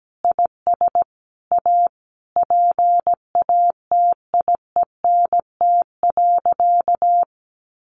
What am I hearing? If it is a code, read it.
Morse code, heard as a keyed tone, about 17 wpm, 700 Hz. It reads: IS A PATIENT.
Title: Additional Easy Phrases 17wpm